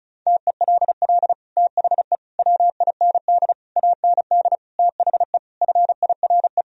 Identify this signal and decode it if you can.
Morse code, heard as a keyed tone, about 34 wpm, 700 Hz. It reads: TELL THE WIND AND THE FIRE
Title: Sentences from Top 1000 Words - Review Part 2 - 35wpm